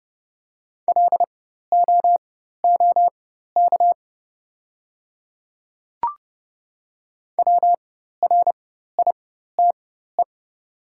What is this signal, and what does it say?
Morse code, heard as a keyed tone, about 30 wpm, spaced as for 14 wpm, 700 Hz. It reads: LOOK WRITE
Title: 4x Character Spacing - Top 100 Words - 30wpm